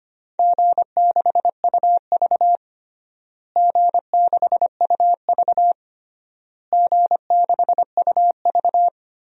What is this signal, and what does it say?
Morse code, heard as a keyed tone, about 25 wpm, 700 Hz. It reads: G6UV G6UV G6UV
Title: DXCC-100 Call Signs - 4 Characters - Repeated 3x - 25wpm